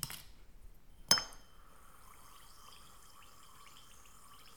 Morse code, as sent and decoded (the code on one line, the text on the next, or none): none